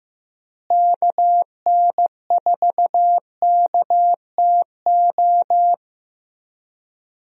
Morse code, heard as a keyed tone, about 16 wpm, 700 Hz. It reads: KN4KTO
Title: USA Call Signs - 6 Characters - Repeated 3x - 15wpm